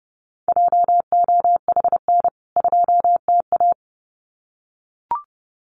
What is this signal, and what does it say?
Morse code, heard as a keyed tone, about 30 wpm, 700 Hz. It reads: JOHN 2TA
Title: CWT Exchange 30wpm